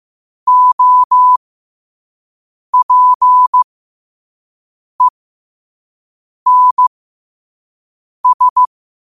Morse code, heard as a keyed tone, about 15 wpm, spaced as for 5 wpm, 1000 Hz. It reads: OPENS
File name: qotdfiles / 5wpm